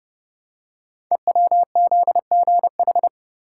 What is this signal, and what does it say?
Morse code, heard as a keyed tone, about 28 wpm, 700 Hz. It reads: EWZGH
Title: Code Groups - 5 Letters - 30wpm